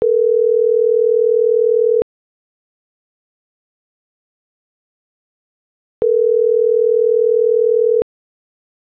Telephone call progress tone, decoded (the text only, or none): ringback tone